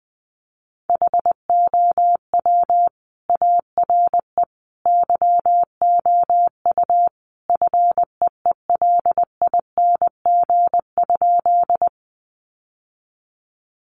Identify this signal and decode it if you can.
Morse code, heard as a keyed tone, about 20 wpm, 700 Hz. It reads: HOW ARE YOU FEELING?
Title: Sentences from Top 1200 Words 20wpm